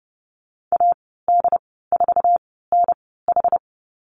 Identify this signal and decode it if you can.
Morse code, heard as a keyed tone, about 30 wpm, spaced as for 17 wpm, 700 Hz. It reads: AD4NH
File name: Lesson 15g - Number 5 - Call Signs 30wpm